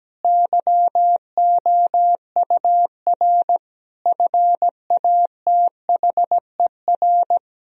Morse code, heard as a keyed tone, about 17 wpm, 700 Hz. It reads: YOUR FATHER
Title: Sentences from Top 1000 Words - Review Part 3 - 17wpm